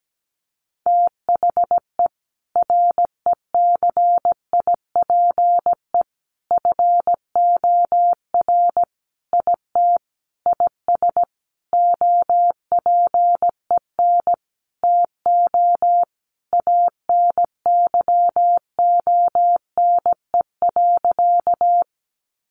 Morse code, heard as a keyed tone, about 17 wpm, 700 Hz. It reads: THE RECIPE FOR IT IS OPEN TO ANYONE.
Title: Sentences from Top 1200 Words 17wpm